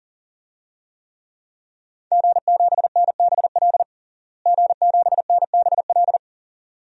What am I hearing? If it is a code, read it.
Morse code, heard as a keyed tone, about 40 wpm, 700 Hz. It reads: G7NBL G7NBL
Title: DXCC-100 Call Signs - 5 Characters - Repeated 2x - 40wpm